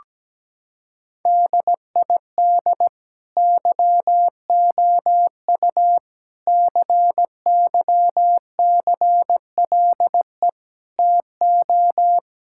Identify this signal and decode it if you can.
Morse code, heard as a keyed tone, about 17 wpm, 700 Hz. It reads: DID YOU CYCLE TO
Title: Sentences from Top 1000 Words - Review Part 6 - 17wpm